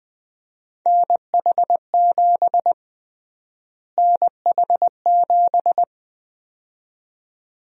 Morse code, heard as a keyed tone, about 20 wpm, 700 Hz. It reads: NH7 NH7
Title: USA Call Sign Prefixes - Repeated 2x - 20wpm